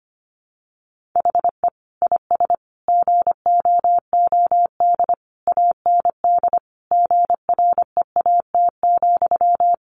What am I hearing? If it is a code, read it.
Morse code, heard as a keyed tone, about 25 wpm, 700 Hz. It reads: HE IS GOOD AND GREAT,
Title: Sentences from Top 1400 Words 25wpm